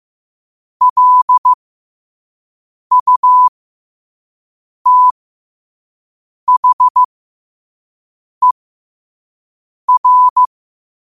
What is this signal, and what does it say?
Morse code, heard as a keyed tone, about 15 wpm, spaced as for 5 wpm, 1000 Hz. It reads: LUTHER
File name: qotdfiles / 5wpm